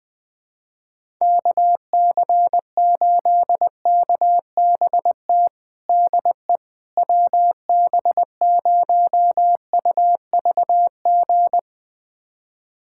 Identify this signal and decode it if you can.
Morse code, heard as a keyed tone, about 20 wpm, 700 Hz. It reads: KC8KBT DE WB0UVG
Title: USA Callsign DE Callsign - 6 Characters - 20wpm